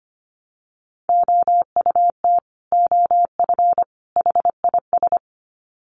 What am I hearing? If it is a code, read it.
Morse code, heard as a keyed tone, about 25 wpm, 700 Hz. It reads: OUT OF HIS
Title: Sets of 3 Words 25wpm